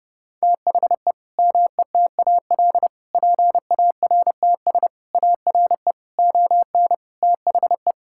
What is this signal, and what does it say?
Morse code, heard as a keyed tone, about 32 wpm, 700 Hz. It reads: THE METAL PARTS ARE ON THE
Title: Sentences from Top 1000 Words - Review Part 6 - 30wpm